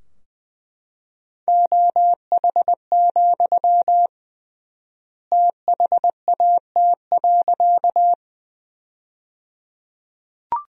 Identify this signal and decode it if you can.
Morse code, heard as a keyed tone, about 21 wpm, 700 Hz. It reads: OH, THAT.
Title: 3x Word-Spacing - Sentences from Top 500 Words - 20wpm